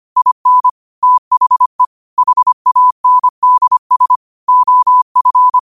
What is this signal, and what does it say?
Morse code, heard as a keyed tone, about 25 wpm, 1000 Hz. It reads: IN THE HANDS OF
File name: qotdfiles / 25wpm